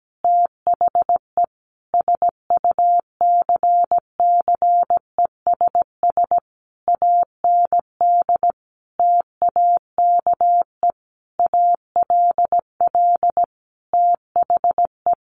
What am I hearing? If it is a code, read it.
Morse code, heard as a keyed tone, about 17 wpm, 700 Hz. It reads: THE SUCCESS AND TAKE ALL THE
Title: Sentences from Top 900 Words 17wpm